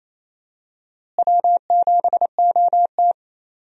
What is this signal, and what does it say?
Morse code, heard as a keyed tone, about 28 wpm, 700 Hz. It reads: W7OT
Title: USA Call Signs - 4 Characters - Repeated 2x - 28wpm